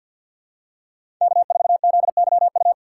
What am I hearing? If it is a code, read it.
Morse code, heard as a keyed tone, about 53 wpm, 700 Hz. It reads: K4ZYU
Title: USA Call Signs - 5 Characters - 4x Character Spacing - 50wpm